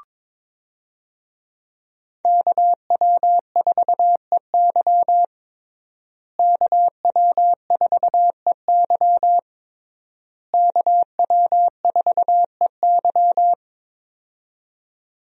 Morse code, heard as a keyed tone, about 22 wpm, 700 Hz. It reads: KW4EY KW4EY KW4EY
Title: USA Call Signs - 5 Characters - Repeated 3x - 22wpm